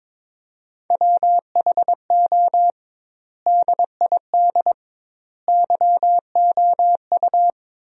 Morse code, heard as a keyed tone, about 22 wpm, 700 Hz. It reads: WHO DID YOU
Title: 2x Word-Spacing - Sentences from Top 300 Words - 22wpm